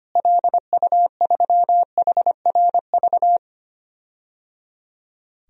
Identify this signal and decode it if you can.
Morse code, heard as a keyed tone, about 25 wpm, 700 Hz. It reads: LU3HRV
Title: DXCC-100 Call Signs - 6 Characters - Repeated 3x - 25wpm